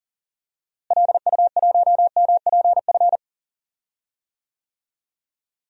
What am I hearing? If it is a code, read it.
Morse code, heard as a keyed tone, about 40 wpm, 700 Hz. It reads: LU1MPF
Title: DXCC-100 Call Signs - 6 Characters - Repeated 2x - 40wpm